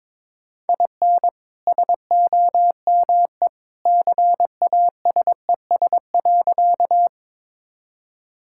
Morse code, heard as a keyed tone, about 22 wpm, 700 Hz. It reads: IN SOME CASES.
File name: practice / 22wpm / Sentences with 2-Word Phrases - 22wpm